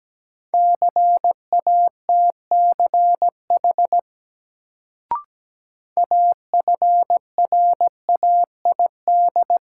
Morse code, heard as a keyed tone, about 17 wpm, 700 Hz. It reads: CATCH AFRAID
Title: Sentences from Top 1500 Words - New Word Drill Alt - 17wpm